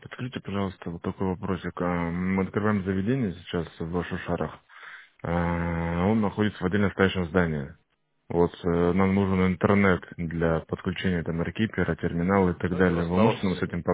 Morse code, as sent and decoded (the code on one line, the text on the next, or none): none